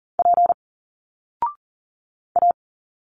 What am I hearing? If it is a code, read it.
Morse code, heard as a keyed tone, about 40 wpm, 700 Hz. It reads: P A
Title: Single Letters 40wpm